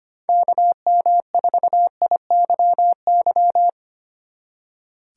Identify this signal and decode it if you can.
Morse code, heard as a keyed tone, about 32 wpm, 700 Hz. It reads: KM4IYY